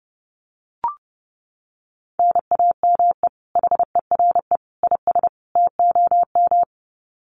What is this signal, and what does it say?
Morse code, heard as a keyed tone, about 28 wpm, 700 Hz. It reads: NAME HERE IS TOM
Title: Common QSO Phrases 30wpm